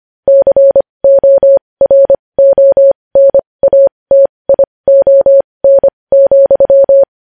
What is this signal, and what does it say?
Morse code, heard as a keyed tone, about 25 wpm, spaced as for 20 wpm, 550 Hz. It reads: CORONATION,
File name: podcast / News headlines-1x-25x20